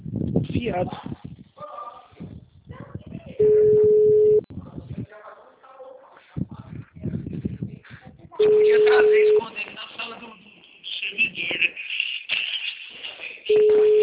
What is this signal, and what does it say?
Ringback tone (Europe, 425 Hz), with speech.